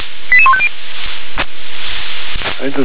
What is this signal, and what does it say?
Five-tone selective call (ZVEI), heard as quick runs of five tones, tones of about 70 ms. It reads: 88150